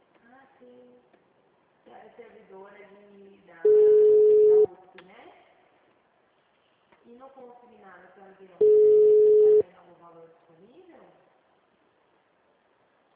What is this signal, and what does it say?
Ringback tone (Europe, 425 Hz), with speech.
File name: out- -2022- - - .10212